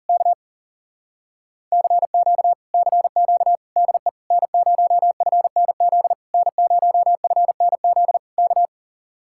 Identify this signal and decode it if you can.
Morse code, heard as a keyed tone, about 40 wpm, 700 Hz. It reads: K CQ CQ DE N0FNZ N0FNZ K